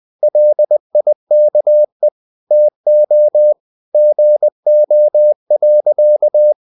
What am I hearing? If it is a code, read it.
Morse code, heard as a keyed tone, about 20 wpm, 600 Hz. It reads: LIKE TO GO.
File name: the old man and the sea / the old man and the sea part 1 20wpm